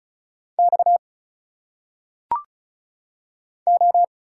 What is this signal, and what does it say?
Morse code, heard as a keyed tone, about 36 wpm, 700 Hz. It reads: X O
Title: Single Letters - Phonetic Pronunciation 35wpm